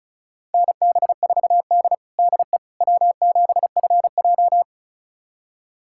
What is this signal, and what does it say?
Morse code, heard as a keyed tone, about 34 wpm, 700 Hz. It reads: NB4D DE W7FJ
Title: USA Callsign DE Callsign - 4 Characters - 35wpm